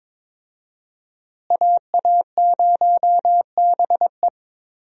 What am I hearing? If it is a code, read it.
Morse code, heard as a keyed tone, about 22 wpm, 700 Hz. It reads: AA0BE
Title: USA Call Signs - 5 Characters - Repeated 3x - 22wpm